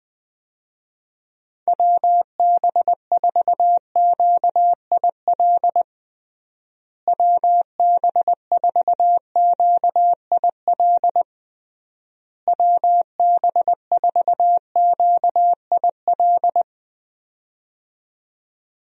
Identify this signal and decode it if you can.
Morse code, heard as a keyed tone, about 20 wpm, 700 Hz. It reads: WB4QIL WB4QIL WB4QIL
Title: USA Call Signs - 6 Characters - Repeated 3x - 20wpm